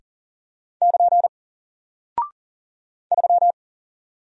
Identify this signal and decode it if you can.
Morse code, heard as a keyed tone, about 40 wpm, 700 Hz. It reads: <KN> 3
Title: Single Character (Letter:Number:Punctuation:Prosign) Alt 40wpm